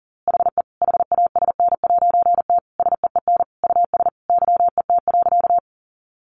Morse code, heard as a keyed tone, about 40 wpm, 700 Hz. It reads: HE HASN'T SEEN US YET.
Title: A Sound Of Thunder 40wpm